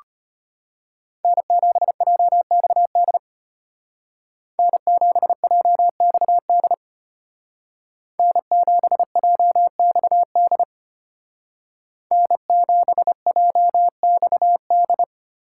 Morse code, heard as a keyed tone, about 33 wpm, 700 Hz. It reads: N7JXD N7JXD N7JXD N7JXD